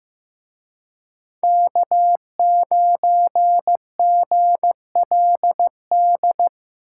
Morse code, heard as a keyed tone, about 18 wpm, 700 Hz. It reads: K9GLD
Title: Speed Racing - USA 1x3 Call Signs 15wpm